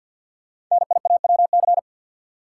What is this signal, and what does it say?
Morse code, heard as a keyed tone, about 48 wpm, 700 Hz. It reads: NIAWC
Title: Code Groups - 5 Letters - 50wpm